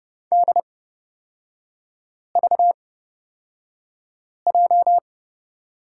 Morse code, heard as a keyed tone, about 31 wpm, spaced as for 5 wpm, 690 Hz. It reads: DVJ
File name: Lesson 37e - Letter x - Three Characters 30wpm